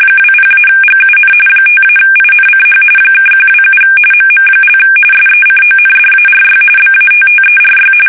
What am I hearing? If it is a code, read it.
RTTY, about 45.45 baud. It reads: 20-GUN 5-INCH (127 MM) SECONDARY BATTERY
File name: fsk850 / battleships